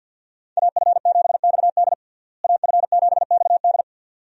Morse code, heard as a keyed tone, about 53 wpm, 700 Hz. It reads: AF7XD AF7XD
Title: USA Call Signs - 5 Characters - Repeated 2x - 50wpm